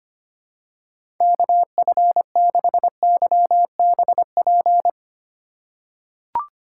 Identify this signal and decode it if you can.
Morse code, heard as a keyed tone, about 25 wpm, 700 Hz. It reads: KF6YBP
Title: USA Call Signs - 3x Character Spacing - 25wpm